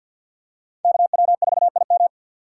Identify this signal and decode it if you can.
Morse code, heard as a keyed tone, about 52 wpm, 700 Hz. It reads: KW4IM
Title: USA Call Signs - 5 Characters - 4x Character Spacing - 50wpm